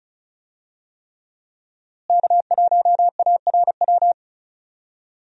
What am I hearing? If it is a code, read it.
Morse code, heard as a keyed tone, about 34 wpm, 700 Hz. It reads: K1ARW
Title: USA Call Signs - 5 Characters - Repeated 3x - 35wpm